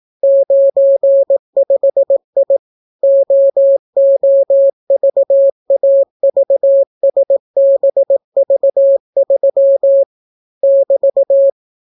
Morse code, heard as a keyed tone, about 18 wpm, 550 Hz. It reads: 95I OOVAVSBV3 <BT>